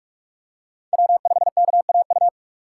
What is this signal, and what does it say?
Morse code, heard as a keyed tone, about 43 wpm, 700 Hz. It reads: W5KAU
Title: USA Call Signs - 5 Characters - 3x Character Spacing - 45wpm